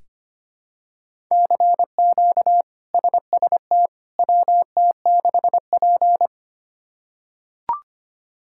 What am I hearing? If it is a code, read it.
Morse code, heard as a keyed tone, about 25 wpm, 700 Hz. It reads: CQ SST WT6P
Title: K1USN SST QSO Practice - 25wpm